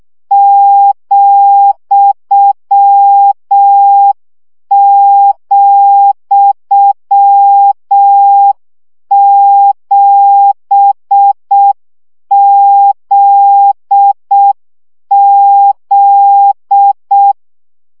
Morse code, heard as a keyed tone, about 6 wpm, 800 Hz. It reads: ,,7ZZ
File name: CW gyakorlas z 7 vesszo 5 baud